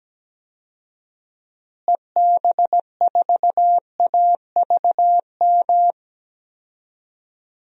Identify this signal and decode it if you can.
Morse code, heard as a keyed tone, about 17 wpm, 700 Hz. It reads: EB4AVM
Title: DXCC-100 Call Signs - 6 Characters - Repeated 2x - 17wpm